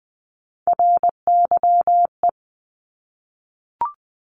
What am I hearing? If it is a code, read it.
Morse code, heard as a keyed tone, about 20 wpm, 700 Hz. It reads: RYE
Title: Multi-letter Phonemes Repeated 3x 20wpm